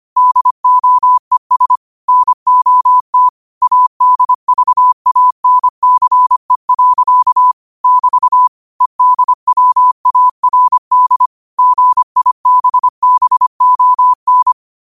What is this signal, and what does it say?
Morse code, heard as a keyed tone, about 25 wpm, 1000 Hz. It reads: DOES NOT ADVANCE. <BT> EDWARD GIBBON
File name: qotdfiles / 25wpm